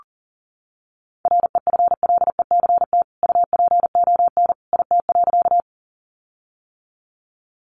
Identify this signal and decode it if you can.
Morse code, heard as a keyed tone, about 40 wpm, 700 Hz. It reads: REFLECT UPON IT.
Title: Sentences from Top 1500 Words 40wpm